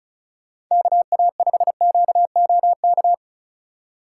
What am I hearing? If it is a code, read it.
Morse code, heard as a keyed tone, about 44 wpm, 700 Hz. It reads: KA5QOK